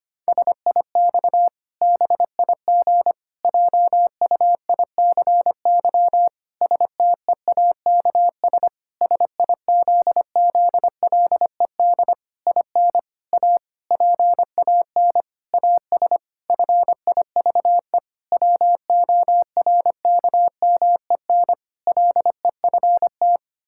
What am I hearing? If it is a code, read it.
Morse code, heard as a keyed tone, about 25 wpm, 700 Hz. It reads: SIX BIG JUICY STEAKS SIZZLED IN A PAN AS FIVE WORKMEN LEFT
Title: Pangram Challenge v2 - 25wpm